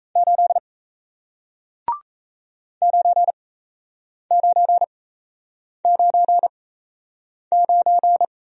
Morse code, heard as a keyed tone, about 38 wpm, spaced as for 8 wpm, 700 Hz. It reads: 8 9999